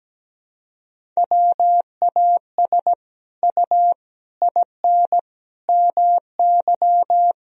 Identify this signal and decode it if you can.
Morse code, heard as a keyed tone, about 17 wpm, 700 Hz. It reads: WAS U IN MY
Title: Sentences from Top 300 Words and Abbreviations 17wpm